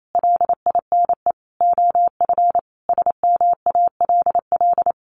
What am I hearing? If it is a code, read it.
Morse code, heard as a keyed tone, about 28 wpm, 700 Hz. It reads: LINE OF SMALL
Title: A Sound Of Thunder 28wpm